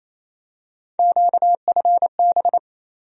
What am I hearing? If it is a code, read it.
Morse code, heard as a keyed tone, about 27 wpm, 700 Hz. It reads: QFB
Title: USA Call Sign Postfixes - 3 Letters 28wpm